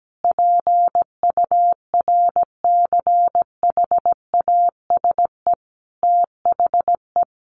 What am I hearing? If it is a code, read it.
Morse code, heard as a keyed tone, about 17 wpm, 700 Hz. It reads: PURCHASE THE
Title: Sentences from Top 800 Words 17wpm